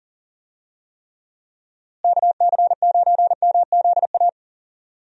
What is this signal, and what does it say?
Morse code, heard as a keyed tone, about 40 wpm, 700 Hz. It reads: KC9MZA
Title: USA Call Signs - 6 Characters - Repeated 3x - 40wpm